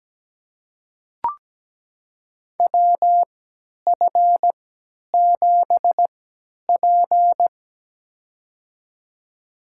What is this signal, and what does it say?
Morse code, heard as a keyed tone, about 17 wpm, spaced as for 10 wpm, 700 Hz. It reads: WF7P
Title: USA Call Signs - 4 Characters - 3x Character Spacing - 17wpm